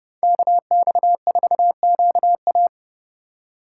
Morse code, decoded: KX4QA

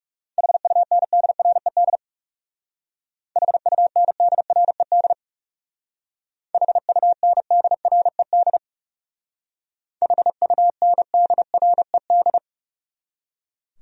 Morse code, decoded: HUNDRED HUNDRED HUNDRED HUNDRED

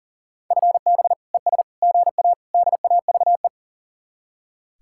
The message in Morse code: FB ES GA DAVE

